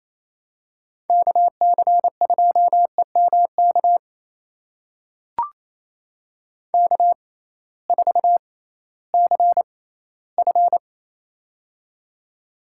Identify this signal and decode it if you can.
Morse code, heard as a keyed tone, about 28 wpm, spaced as for 16 wpm, 700 Hz. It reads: KC2EMK K4CF
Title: USA Call Signs - 6x Character Spacing - 28wpm